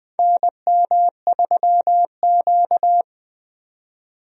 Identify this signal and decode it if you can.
Morse code, heard as a keyed tone, about 21 wpm, 700 Hz. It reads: NM3Q
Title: USA Call Signs - 4 Characters - 20wpm